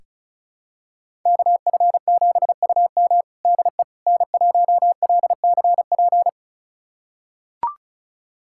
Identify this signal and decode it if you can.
Morse code, heard as a keyed tone, about 36 wpm, 700 Hz. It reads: KF7UM DE N1LCP
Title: USA Callsign DE Callsign - 5 Characters - 35wpm